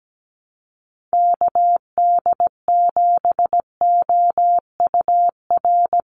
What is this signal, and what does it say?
Morse code, heard as a keyed tone, about 17 wpm, 700 Hz. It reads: KD7OUR
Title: USA Call Signs 17wpm